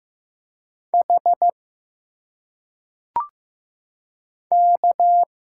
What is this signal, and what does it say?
Morse code, heard as a keyed tone, about 15 wpm, 700 Hz. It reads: H K
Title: Single Letters - Phonetic Pronunciation 15wpm